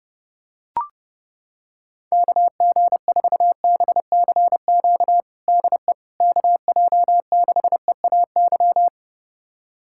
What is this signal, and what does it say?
Morse code, heard as a keyed tone, about 30 wpm, 700 Hz. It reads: KG4BCQ DE KJ6EAY
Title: USA Callsign DE Callsign - 6 Characters - 30wpm